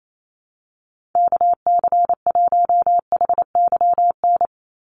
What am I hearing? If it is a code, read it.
Morse code, heard as a keyed tone, about 28 wpm, 700 Hz. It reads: KC1HYN
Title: USA Call Signs 28wpm